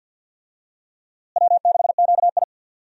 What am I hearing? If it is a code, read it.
Morse code, heard as a keyed tone, about 52 wpm, 700 Hz. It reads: W6QI
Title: USA Call Signs - 4 Characters - Repeated 3x - 50wpm